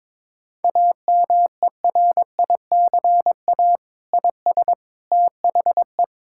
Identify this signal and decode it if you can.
Morse code, heard as a keyed tone, about 22 wpm, 700 Hz. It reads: AMERICA IS THE
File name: practice / 22wpm / Sentences with 8-Word Phrases - 22wpm